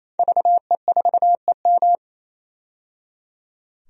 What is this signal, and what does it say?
Morse code, heard as a keyed tone, about 27 wpm, 700 Hz. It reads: VE4EM
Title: DXCC-100 Call Signs - 5 Characters - 28wpm